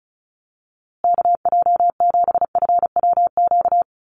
Morse code, heard as a keyed tone, about 34 wpm, 700 Hz. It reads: KJ7FWQ
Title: USA Call Signs Repeated 2x 35wpm